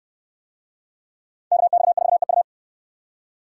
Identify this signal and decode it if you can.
Morse code, heard as a keyed tone, about 66 wpm, 700 Hz. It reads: KX4EU